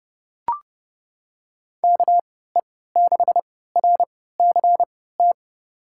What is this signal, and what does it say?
Morse code, heard as a keyed tone, about 30 wpm, spaced as for 17 wpm, 700 Hz. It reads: KE6RCT
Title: USA Call Signs - 3x Character Spacing - 30wpm